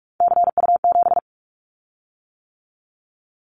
Morse code, heard as a keyed tone, about 46 wpm, 700 Hz. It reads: CU7
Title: Callsign Prefixes for Top 50 DXCC - 45wpm